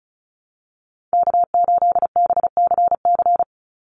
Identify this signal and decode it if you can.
Morse code, heard as a keyed tone, about 34 wpm, 700 Hz. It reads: K8BCC
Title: USA Call Signs 35wpm